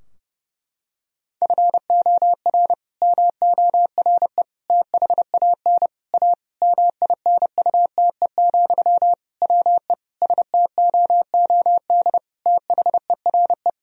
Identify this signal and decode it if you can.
Morse code, heard as a keyed tone, about 30 wpm, 700 Hz. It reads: FOR MORE THAN A MINUTE, WE STOOD THERE
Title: Sentences from Top 1000 Words - Review Part 2 - 30wpm